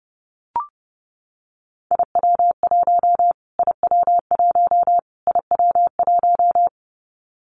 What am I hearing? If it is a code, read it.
Morse code, heard as a keyed tone, about 28 wpm, 700 Hz. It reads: IW1 IW1 IW1
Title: Callsign Prefixes for Top 25 DXCC - Repeated 3x - 30wpm